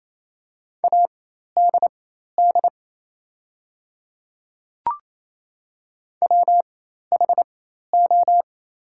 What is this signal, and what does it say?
Morse code, heard as a keyed tone, about 28 wpm, spaced as for 13 wpm, 700 Hz. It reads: ADD WHO
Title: 4x Character Spacing - Top 300 Words - 28wpm